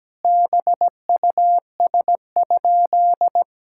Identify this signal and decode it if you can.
Morse code, heard as a keyed tone, about 17 wpm, 700 Hz. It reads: BUS?
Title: Sentences from Top 1000 Words - Review Part 6 - 17wpm